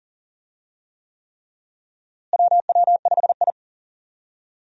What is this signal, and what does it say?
Morse code, heard as a keyed tone, about 40 wpm, 700 Hz. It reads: WW5I
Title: USA Call Signs - 4 Characters - Repeated 3x - 40wpm